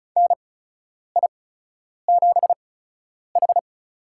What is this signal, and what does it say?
Morse code, heard as a keyed tone, about 36 wpm, spaced as for 10 wpm, 700 Hz. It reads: NI7H
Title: USA Call Signs - 8x Character Spacing - 35wpm